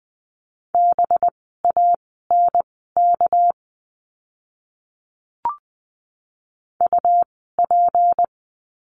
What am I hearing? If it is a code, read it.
Morse code, heard as a keyed tone, about 20 wpm, spaced as for 14 wpm, 700 Hz. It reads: BANK UP
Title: 199 Lesson 32f - Letter k - Words 20wpm